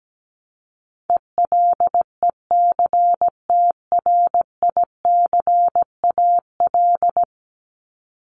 Morse code, heard as a keyed tone, about 17 wpm, 700 Hz. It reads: ELECTRICAL
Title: Top 751-1000 Words 17wpm